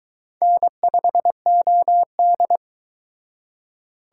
Morse code, heard as a keyed tone, about 22 wpm, 700 Hz. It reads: N5OD